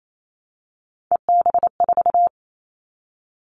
Morse code, heard as a keyed tone, about 28 wpm, 700 Hz. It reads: EB4
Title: Callsign Prefixes for Top 25 DXCC - Repeated 3x - 28wpm